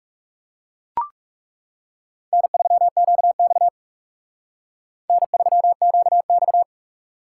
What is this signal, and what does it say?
Morse code, heard as a keyed tone, about 38 wpm, 700 Hz. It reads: N3QX N3QX
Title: Speed Racing - USA Call Signs 30wpm